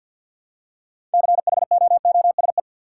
Morse code, heard as a keyed tone, about 51 wpm, 700 Hz. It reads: CHOOSE
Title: Six Letter Words - 50wpm